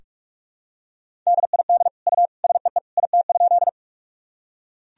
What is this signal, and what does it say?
Morse code, heard as a keyed tone, about 42 wpm, 700 Hz. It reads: DID U SEE IT?
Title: Sentences from Top 100 Words and Abbreviations 45wpm